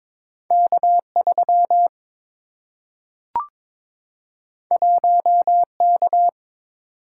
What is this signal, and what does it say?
Morse code, heard as a keyed tone, about 22 wpm, 700 Hz. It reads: K3 1K
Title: Single-Letter Single-Number 22wpm